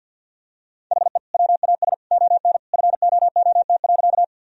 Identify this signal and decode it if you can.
Morse code, heard as a keyed tone, about 47 wpm, 700 Hz. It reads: HE WAS ON FOOT.
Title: Sentences from Top 1000 Words - Review Part 3 - 50wpm